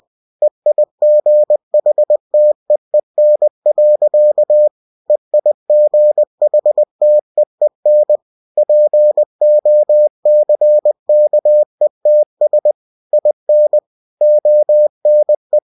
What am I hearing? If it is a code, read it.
Morse code, heard as a keyed tone, about 20 wpm, 600 Hz. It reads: EIGHTEEN. EIGHTEEN POCKETS IN ONE
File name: cw-books / winnie-the-pooh / winnie-20wpm-page-0060